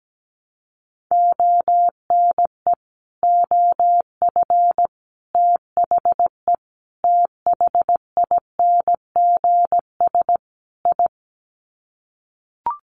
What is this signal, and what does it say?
Morse code, heard as a keyed tone, about 17 wpm, 700 Hz. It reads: ONE OF THE THINGS I
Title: Sets of 5 Words 17wpm